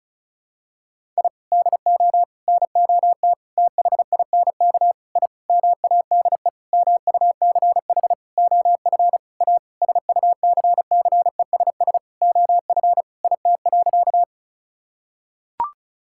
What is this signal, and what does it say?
Morse code, heard as a keyed tone, about 34 wpm, 700 Hz. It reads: I DO NOT THINK I MADE MUCH OF A SUCCESS OF IT.
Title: Sentences from Top 1000 Words - Review Part 6 - 35wpm